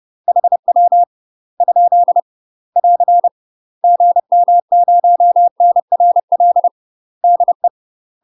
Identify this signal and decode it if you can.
Morse code, heard as a keyed tone, about 30 wpm, 700 Hz. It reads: HW ? <AR> GM0NRL DE
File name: UK MASTER QSOs TRAINING / 30WPM / UK MASTER QSOs 0041